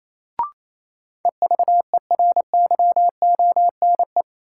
Morse code, heard as a keyed tone, about 28 wpm, 700 Hz. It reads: EVERYONE
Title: Sentences from Top 800 Words - New Word Drill - 28wpm